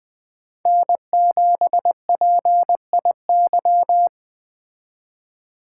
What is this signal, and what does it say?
Morse code, heard as a keyed tone, about 21 wpm, 700 Hz. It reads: N7PIY